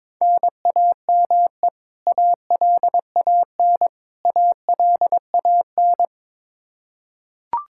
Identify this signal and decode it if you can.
Morse code, heard as a keyed tone, about 22 wpm, 700 Hz. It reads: NAME ALAN ALAN
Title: Practice QSO - Exchange 1 - Beginning - 22wpm